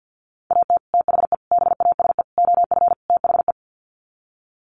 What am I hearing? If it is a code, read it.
Morse code, heard as a keyed tone, about 50 wpm, 700 Hz. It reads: AT THE BASE OF THE
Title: Sets of 5 Words 50wpm